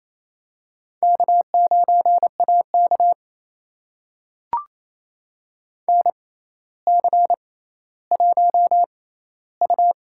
Morse code, heard as a keyed tone, about 28 wpm, spaced as for 14 wpm, 700 Hz. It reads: K9AK NC1U
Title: USA Call Signs - 4 Characters - 6x Character Spacing - 28wpm